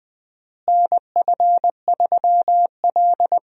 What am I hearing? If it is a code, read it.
Morse code, heard as a keyed tone, about 21 wpm, 700 Hz. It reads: NF3L
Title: USA Call Signs - 4 Characters - Repeated 2x - 20wpm